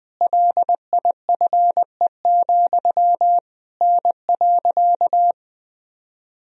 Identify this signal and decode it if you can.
Morse code, heard as a keyed tone, about 20 wpm, 700 Hz. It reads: LIFE, N.
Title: Sentences from Top 300 Words and Abbreviations 20wpm